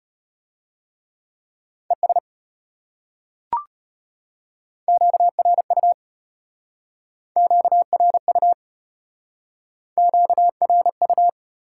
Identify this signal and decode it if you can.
Morse code, heard as a keyed tone, about 33 wpm, 700 Hz. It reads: ES QRU QRU QRU